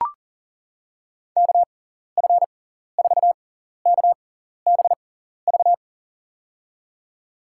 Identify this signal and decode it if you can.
Morse code, heard as a keyed tone, about 40 wpm, spaced as for 14 wpm, 700 Hz. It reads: KF4KBV
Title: USA Call Signs - 6x Character Spacing - 40wpm